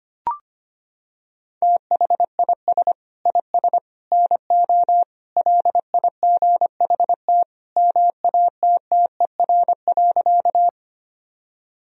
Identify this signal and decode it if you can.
Morse code, heard as a keyed tone, about 25 wpm, 700 Hz. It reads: THIS IS NO LIGHT MATTER.
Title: Sentences from Top 1000 Words - Review Part 1 - 25wpm